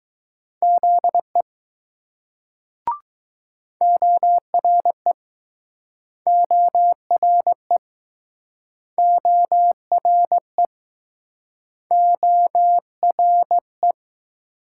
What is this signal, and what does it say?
Morse code, heard as a keyed tone, about 20 wpm, 700 Hz. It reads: ZE ORE ORE ORE ORE